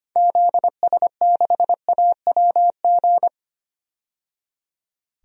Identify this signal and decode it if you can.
Morse code, heard as a keyed tone, about 25 wpm, 700 Hz. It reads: ZS6AWG
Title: DXCC-100 Call Signs - 6 Characters - 25wpm